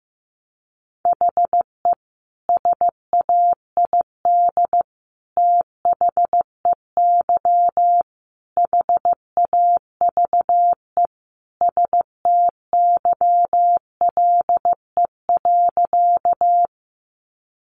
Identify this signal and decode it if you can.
Morse code, heard as a keyed tone, about 15 wpm, 700 Hz. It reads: HE SAID THEY HAVE STYLE.
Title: Sentences from Top 800 Words 15wpm